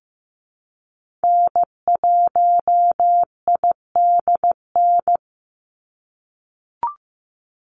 Morse code, heard as a keyed tone, about 16 wpm, 700 Hz. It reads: N1IDN
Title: NA SOTA Activator Callsigns 15wpm